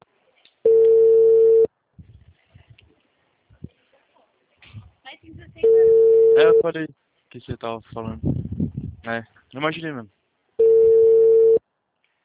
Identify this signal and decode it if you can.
Ringback tone (China, 450 Hz), with speech.